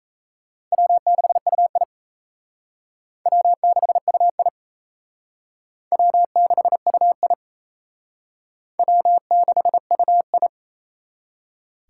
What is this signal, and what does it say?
Morse code, heard as a keyed tone, about 34 wpm, 700 Hz. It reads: W6UI W6UI W6UI W6UI
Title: Speed Racing - USA 1x2 Call Signs 28wpm